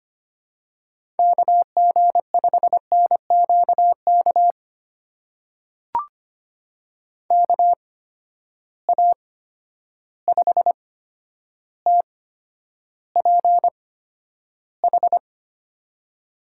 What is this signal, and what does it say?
Morse code, heard as a keyed tone, about 25 wpm, spaced as for 11 wpm, 700 Hz. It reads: KG5NQK KA5TPH
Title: USA Call Signs - 6 Characters - 8x Character Spacing - 25wpm